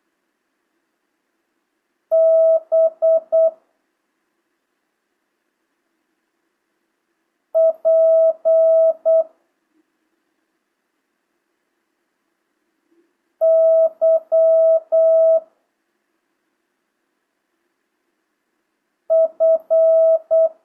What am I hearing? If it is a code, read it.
Morse code, heard as a keyed tone, about 8 wpm, 650 Hz. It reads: B P Y F
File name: jh8cbh / 2025cw